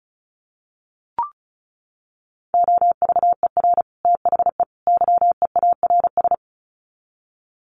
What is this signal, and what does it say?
Morse code, heard as a keyed tone, about 34 wpm, 700 Hz. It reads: OVER THE YEARS
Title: Sets of 3 Words 35wpm